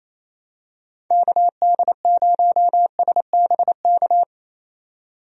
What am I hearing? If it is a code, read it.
Morse code, heard as a keyed tone, about 27 wpm, 700 Hz. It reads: KD0SBK
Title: USA Call Signs - 6 Characters - 28wpm